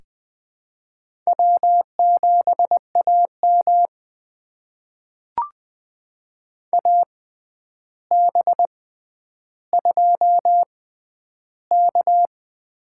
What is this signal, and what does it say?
Morse code, heard as a keyed tone, about 20 wpm, spaced as for 10 wpm, 700 Hz. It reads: W7AM AB2K
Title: USA Call Signs - 4 Characters - 6x Character Spacing - 20wpm